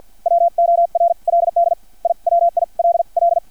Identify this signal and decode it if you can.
Morse code, heard as a keyed tone, about 50 wpm, spaced as for 40 wpm, 680 Hz. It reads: WORLD IWILL
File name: archived / CRSnet